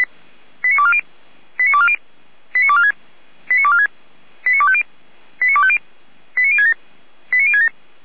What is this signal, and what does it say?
Five-tone selective call (ZVEI), heard as quick runs of five tones, tones of about 70 ms. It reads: 89240 89240 89246 89246 89250 89250 89067 89067
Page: /455/SDR/ZVEI/89160/